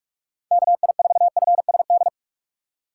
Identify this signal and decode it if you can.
Morse code, heard as a keyed tone, about 46 wpm, 700 Hz. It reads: KI4FSD
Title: USA Call Signs - 6x Character Spacing - 45wpm